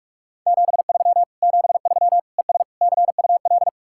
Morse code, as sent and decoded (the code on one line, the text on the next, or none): --... ...-- / --... ...-- / . ... / -.-. ..- .-..
73 73 ES CUL